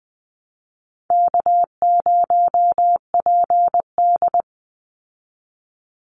Morse code, heard as a keyed tone, about 20 wpm, 700 Hz. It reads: K0PD
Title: Easy Call Signs Repeated 3x - 20wpm